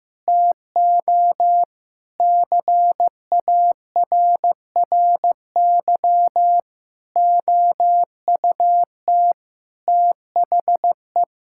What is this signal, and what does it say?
Morse code, heard as a keyed tone, about 15 wpm, 700 Hz. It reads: TO CARRY OUT THE
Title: Sentences from Top 1600 Words 15wpm